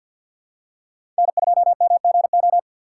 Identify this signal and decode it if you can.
Morse code, heard as a keyed tone, about 50 wpm, 700 Hz. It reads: N2MGO
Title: USA Call Signs - 5 Characters - 2x Character Spacing - 50wpm